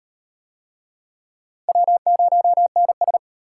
Morse code, heard as a keyed tone, about 38 wpm, 700 Hz. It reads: W0NS